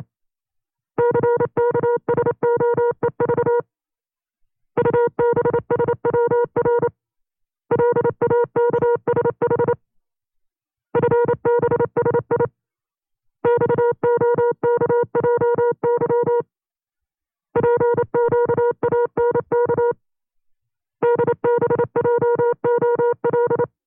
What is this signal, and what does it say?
Morse code, heard as a keyed tone, about 29 wpm, 480 Hz. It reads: CKSOEV UBSWR LAKSH FBSI XOKJY PQANK DBJOL
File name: CW-practice/25WPM